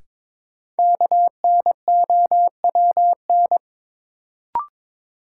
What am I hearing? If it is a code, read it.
Morse code, heard as a keyed tone, about 21 wpm, 700 Hz. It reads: KNOWN
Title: Sentences from Top 600 Words - New Word Drill - 22wpm